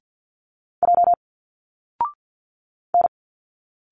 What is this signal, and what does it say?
Morse code, heard as a keyed tone, about 50 wpm, 700 Hz. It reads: J N
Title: Single Letters 50wpm